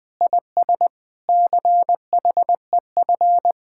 Morse code, heard as a keyed tone, about 20 wpm, 700 Hz. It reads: IS CHEF
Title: Speed-Racing Common QSO Phrases - 17wpm